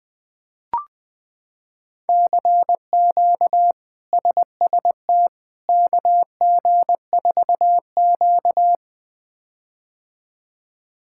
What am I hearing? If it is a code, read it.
Morse code, heard as a keyed tone, about 20 wpm, 700 Hz. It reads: CQ SST KG4Q
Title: K1USN SST QSO Practice - 20wpm